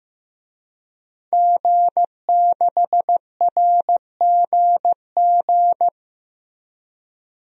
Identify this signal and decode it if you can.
Morse code, heard as a keyed tone, about 15 wpm, 700 Hz. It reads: G6RGG
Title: DXCC-100 Call Signs - 5 Characters - 15wpm